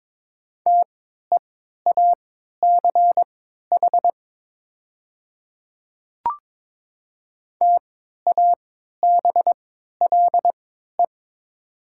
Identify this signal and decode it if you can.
Morse code, heard as a keyed tone, about 22 wpm, spaced as for 12 wpm, 700 Hz. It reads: TEACH TABLE
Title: 3x Character Spacing - Top 500 Words - 22wpm